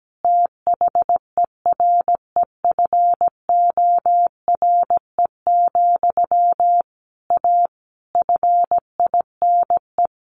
Morse code, heard as a keyed tone, about 17 wpm, 700 Hz. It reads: THEREFORE, A FINE
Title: Sentences from Top 1400 Words 17wpm